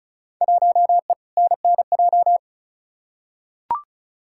Morse code, heard as a keyed tone, about 34 wpm, 700 Hz. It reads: 1E NNJ
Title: Field Day Exchanges - 35wpm